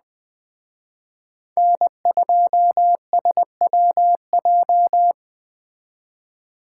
Morse code, heard as a keyed tone, about 20 wpm, 700 Hz. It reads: N2SWJ
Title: Speed Racing - USA 1x3 Call Signs 17wpm